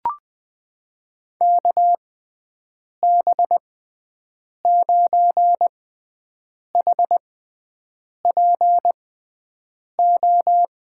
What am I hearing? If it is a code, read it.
Morse code, heard as a keyed tone, about 21 wpm, spaced as for 7 wpm, 700 Hz. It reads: KB9HPO
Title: USA Call Signs - 6x Character Spacing - 20wpm